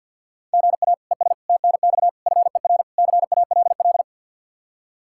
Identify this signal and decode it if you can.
Morse code, heard as a keyed tone, about 50 wpm, 700 Hz. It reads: GA ES TNX FER CALL
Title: Practice QSO - Exchange 1 - Beginning - 50wpm